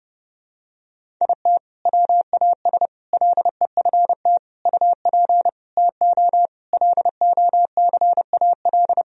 Morse code, heard as a keyed tone, about 30 wpm, 700 Hz. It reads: IT WAS LEFT UP TO LOCAL
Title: Sentences from Top 1000 Words - Review Part 4 - 30wpm